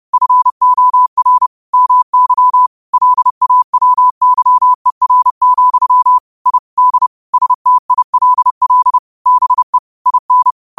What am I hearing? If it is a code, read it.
Morse code, heard as a keyed tone, about 30 wpm, 1000 Hz. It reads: FOR MY LAWYER, I D STILL BE IN
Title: 30 WPM morse code quotes for Thu, 29 Feb 2024